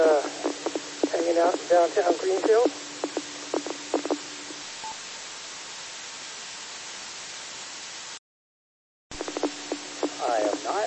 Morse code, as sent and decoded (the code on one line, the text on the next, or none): none